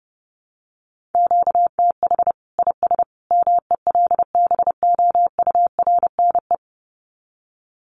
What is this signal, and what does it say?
Morse code, heard as a keyed tone, about 30 wpm, 700 Hz. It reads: QTH IS MELBOURNE
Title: Common QSO Phrases 30wpm